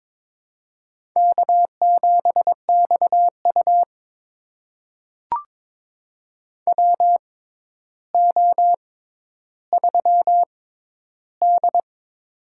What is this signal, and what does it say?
Morse code, heard as a keyed tone, about 22 wpm, spaced as for 11 wpm, 700 Hz. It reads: K7XU WO3D
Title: USA Call Signs - 6x Character Spacing - 22wpm